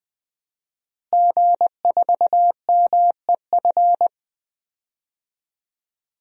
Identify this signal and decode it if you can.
Morse code, heard as a keyed tone, about 20 wpm, 700 Hz. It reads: G4MEF
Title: DXCC-100 Call Signs - 5 Characters - Repeated 2x - 20wpm